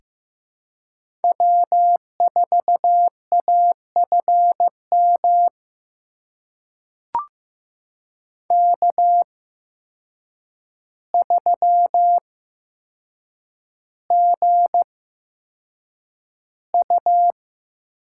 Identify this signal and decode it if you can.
Morse code, heard as a keyed tone, about 15 wpm, spaced as for 7 wpm, 700 Hz. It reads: W4AFM K3GU
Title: USA Call Signs - 5 Characters - 8x Character Spacing - 15wpm